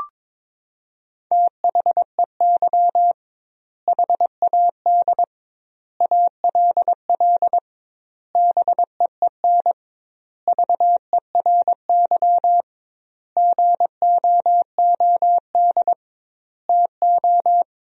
Morse code, heard as a keyed tone, about 22 wpm, 700 Hz. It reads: THEY HAD ALL BEEN VERY GOOD TO
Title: 2x Word-Spacing - Sentences from Top 200 Words - 22wpm